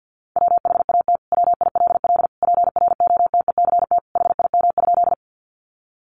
Morse code, heard as a keyed tone, about 49 wpm, 700 Hz. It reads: WHAT WILL PROTECT HIM?
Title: Sentences from Top 900 Words 50wpm